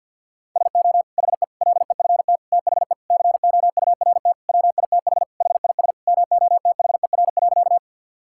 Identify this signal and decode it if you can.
Morse code, heard as a keyed tone, about 50 wpm, 700 Hz. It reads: SO HE LEFT THE COURT WITH HIS MOTHER.
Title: Sentences from Top 1600 Words 50wpm